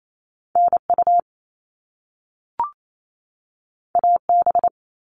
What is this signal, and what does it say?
Morse code, heard as a keyed tone, about 28 wpm, 700 Hz. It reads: NU AB
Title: Canadian Provinces and Territories 28wpm